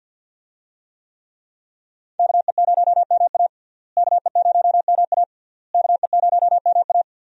Morse code, heard as a keyed tone, about 51 wpm, 700 Hz. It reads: KE0MA KE0MA KE0MA
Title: USA Call Signs - 5 Characters - Repeated 3x - 50wpm